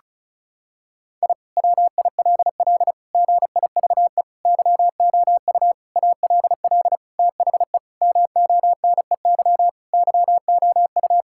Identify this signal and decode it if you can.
Morse code, heard as a keyed tone, about 36 wpm, 700 Hz. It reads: I WILL GIVE YOU ALL THE MONEY YOU
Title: Sentences from Top 1000 Words - Review Part 6 - 35wpm